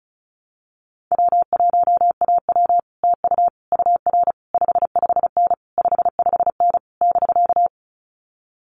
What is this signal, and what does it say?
Morse code, heard as a keyed tone, about 34 wpm, 700 Hz. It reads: W1AW TU UR 55N 55N <BK>
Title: Common QSO Phrases Encore 35wpm